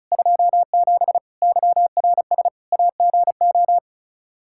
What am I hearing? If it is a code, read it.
Morse code, heard as a keyed tone, about 34 wpm, 700 Hz. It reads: 27 YRS AGO
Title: Practice QSO - Exchange 3 - More Info - 35wpm